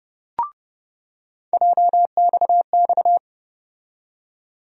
Morse code, heard as a keyed tone, about 30 wpm, 700 Hz. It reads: JXX
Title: USA Call Sign Postfixes - 3 Letters 30wpm